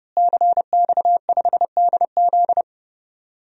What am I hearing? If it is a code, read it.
Morse code, heard as a keyed tone, about 32 wpm, 700 Hz. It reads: CX5DZ
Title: DXCC-100 Call Signs - 5 Characters - Repeated 3x - 30wpm